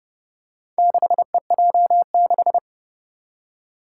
Morse code, heard as a keyed tone, about 32 wpm, 700 Hz. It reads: 6EJ6
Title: Four Challenging Letter-Number Combinations 30wpm